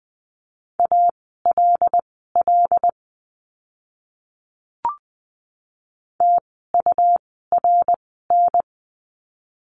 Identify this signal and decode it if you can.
Morse code, heard as a keyed tone, about 20 wpm, spaced as for 14 wpm, 700 Hz. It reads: ALL TURN
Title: 244 Lesson 39f - Letter z - Words 20wpm